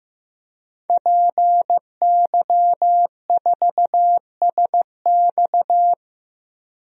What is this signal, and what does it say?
Morse code, heard as a keyed tone, about 15 wpm, 700 Hz. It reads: PY4SX
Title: DXCC-100 Call Signs - 5 Characters - 15wpm